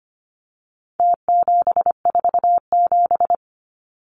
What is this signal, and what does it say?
Morse code, heard as a keyed tone, about 25 wpm, 700 Hz. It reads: T747
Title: Numbers - Contest Style - 4-Digit 25wpm